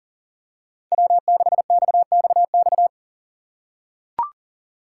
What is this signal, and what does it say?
Morse code, heard as a keyed tone, about 40 wpm, 700 Hz. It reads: W6XXX
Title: USA Call Signs - 5 Characters - 2x Character Spacing - 40wpm